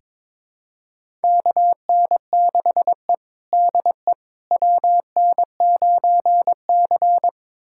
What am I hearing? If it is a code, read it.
Morse code, heard as a keyed tone, about 22 wpm, 700 Hz. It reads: KN6E DE WN9C
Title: USA Callsign DE Callsign - 4 Characters - 22wpm